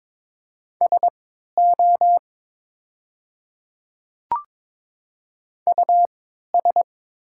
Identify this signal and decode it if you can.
Morse code, heard as a keyed tone, about 22 wpm, spaced as for 12 wpm, 700 Hz. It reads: SO US
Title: 3x Character Spacing - Top 500 Words - 22wpm